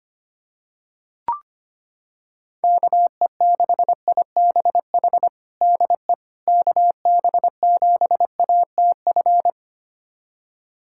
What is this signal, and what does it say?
Morse code, heard as a keyed tone, about 25 wpm, 700 Hz. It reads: KE6IBH DE KB7ATF
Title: USA Callsign DE Callsign - 6 Characters - 25wpm